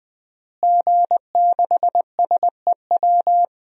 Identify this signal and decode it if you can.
Morse code, heard as a keyed tone, about 20 wpm, 700 Hz. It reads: G6SEW
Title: DXCC-100 Call Signs - 5 Characters - Repeated 2x - 20wpm